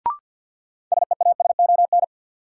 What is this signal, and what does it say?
Morse code, heard as a keyed tone, about 53 wpm, 700 Hz. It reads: SEASON
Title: Sets of 5 Words - New Word Drill - 50wpm